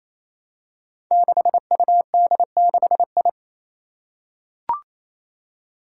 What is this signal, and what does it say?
Morse code, heard as a keyed tone, about 29 wpm, 700 Hz. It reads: 6UD6I
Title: Five Challenging Letter-Number Combinations 28wpm